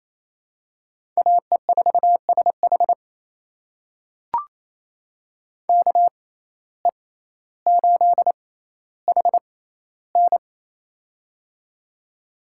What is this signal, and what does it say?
Morse code, heard as a keyed tone, about 28 wpm, spaced as for 14 wpm, 700 Hz. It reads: AE4SH KE8HN
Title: USA Call Signs - 5 Characters - 6x Character Spacing - 28wpm